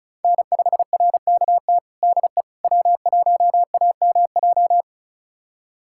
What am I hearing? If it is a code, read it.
Morse code, heard as a keyed tone, about 36 wpm, 700 Hz. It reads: N5RKT DE W1AMJ
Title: USA Callsign DE Callsign - 5 Characters - 35wpm